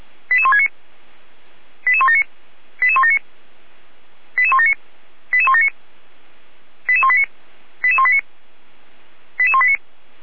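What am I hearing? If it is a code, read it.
Five-tone selective call (ZVEI), heard as quick runs of five tones, tones of about 70 ms. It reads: 88169 88179 88179 88179 88179 88189 88189 88189
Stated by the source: Five-tone ZVEI selective call received on SDR, fire brigade Orlamünde (feuerwehr-orlamuende).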